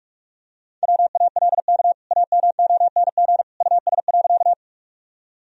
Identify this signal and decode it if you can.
Morse code, heard as a keyed tone, about 41 wpm, 700 Hz. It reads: WALK AMONG US.
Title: Sentences from Top 1000 Words - Review Part 3 - 45wpm